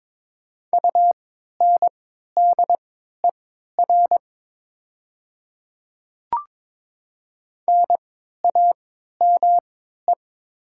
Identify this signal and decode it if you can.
Morse code, heard as a keyed tone, about 22 wpm, spaced as for 12 wpm, 700 Hz. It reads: UNDER NAME
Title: 3x Character Spacing - Top 500 Words - 22wpm